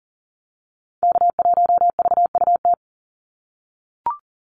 Morse code, heard as a keyed tone, about 40 wpm, 700 Hz. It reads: K1VUT
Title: Call Signs - 40wpm